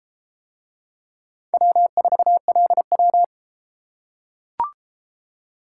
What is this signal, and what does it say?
Morse code, heard as a keyed tone, about 31 wpm, 700 Hz. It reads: W4LW